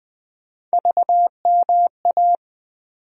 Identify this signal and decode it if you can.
Morse code, heard as a keyed tone, about 19 wpm, 700 Hz. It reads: VMA
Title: USA Call Sign Postfixes - 3 Letters 20wpm